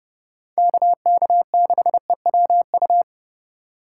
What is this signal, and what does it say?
Morse code, heard as a keyed tone, about 29 wpm, 700 Hz. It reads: KK6EWU